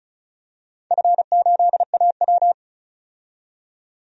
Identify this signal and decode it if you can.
Morse code, heard as a keyed tone, about 36 wpm, 700 Hz. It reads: F8AW
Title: DXCC-100 Call Signs - 4 Characters - 35wpm